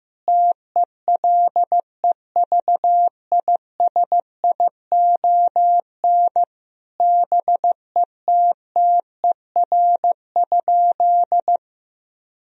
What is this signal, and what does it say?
Morse code, heard as a keyed tone, about 15 wpm, 700 Hz. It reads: TELEVISION BETTER?
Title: Sentences from Top 1000 Words - Review Part 4 - 15wpm